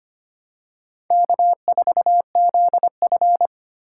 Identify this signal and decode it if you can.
Morse code, heard as a keyed tone, about 25 wpm, 700 Hz. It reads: K4ZF
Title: USA Call Signs - 4 Characters - 3x Character Spacing - 25wpm